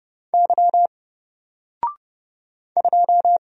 Morse code, heard as a keyed tone, about 30 wpm, 700 Hz. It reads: Y 2
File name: Single-Letter-Number-30wpm